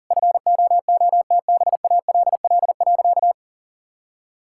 Morse code, heard as a keyed tone, about 40 wpm, 700 Hz. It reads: FOOTBALL.
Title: Sentences from Top 1600 Words 40wpm